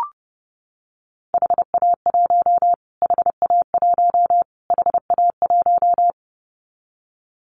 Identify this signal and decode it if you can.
Morse code, heard as a keyed tone, about 30 wpm, 700 Hz. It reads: HA1 HA1 HA1
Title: Callsign Prefixes for Top 25 DXCC - Repeated 3x - 30wpm